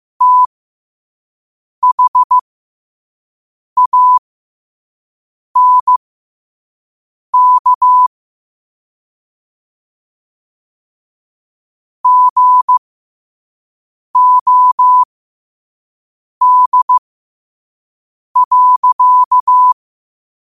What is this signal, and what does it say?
Morse code, heard as a keyed tone, about 15 wpm, spaced as for 5 wpm, 1000 Hz. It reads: THANK GOD.
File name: qotdfiles / 5wpm